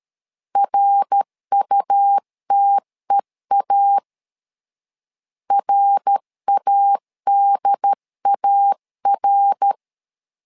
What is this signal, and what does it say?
Morse code, heard as a keyed tone, about 13 wpm, 800 Hz. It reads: RUTEA RADAR
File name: LEZIONI / 3 MWRO / 3 MWR 12wpm 1711